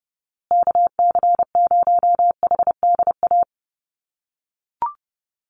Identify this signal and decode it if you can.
Morse code, heard as a keyed tone, about 30 wpm, 700 Hz. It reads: KC0HDA
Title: USA Call Signs Repeated 2x 30wpm